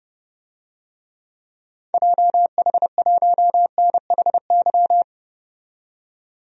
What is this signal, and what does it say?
Morse code, heard as a keyed tone, about 30 wpm, 700 Hz. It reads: JH1NHY
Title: DXCC-100 Call Signs - 6 Characters - 30wpm